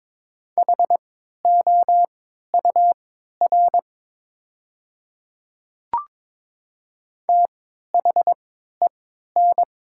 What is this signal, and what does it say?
Morse code, heard as a keyed tone, about 22 wpm, spaced as for 13 wpm, 700 Hz. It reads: HOUR THEN
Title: 3x Character Spacing - Top 500 Words - 22wpm